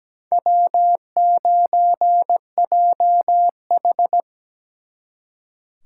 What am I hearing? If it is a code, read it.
Morse code, heard as a keyed tone, about 20 wpm, 700 Hz. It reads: W9JH
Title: Speed Racing - USA 1x2 Call Signs 17wpm